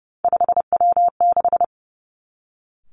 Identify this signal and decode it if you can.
Morse code, heard as a keyed tone, about 30 wpm, 700 Hz. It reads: 5W6
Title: Three Challenging Letter-Number Combinations 30wpm